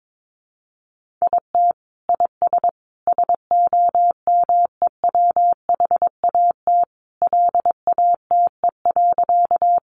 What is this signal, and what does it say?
Morse code, heard as a keyed tone, about 22 wpm, 700 Hz. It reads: IT IS SOMEWHAT LATE.
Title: Sentences from Top 1300 Words 22wpm